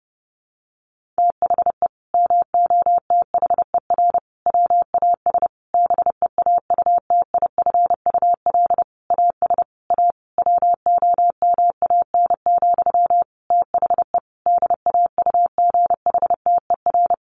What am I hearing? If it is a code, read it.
Morse code, heard as a keyed tone, about 28 wpm, 700 Hz. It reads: THE MOTHER WAS BEAUTIFUL AS A WOMAN, THE DAUGHTER
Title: Sentences from Top 1500 Words 30wpm